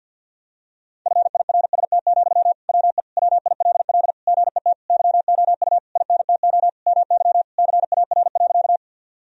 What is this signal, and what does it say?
Morse code, heard as a keyed tone, about 52 wpm, 700 Hz. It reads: FIRST, WE WILL GET YOU INTO MY CAR.
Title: Sentences from Top 1000 Words - Review Part 1 - 50wpm